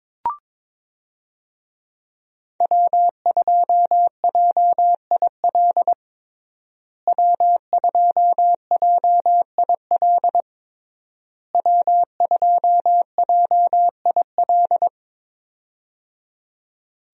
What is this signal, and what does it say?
Morse code, heard as a keyed tone, about 22 wpm, 700 Hz. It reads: W2JIL W2JIL W2JIL
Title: USA Call Signs - 5 Characters - Repeated 3x - 22wpm